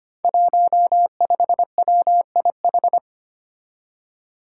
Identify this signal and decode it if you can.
Morse code, heard as a keyed tone, about 25 wpm, 700 Hz. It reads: 15WIH
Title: Five Challenging Letter-Number Combinations 25wpm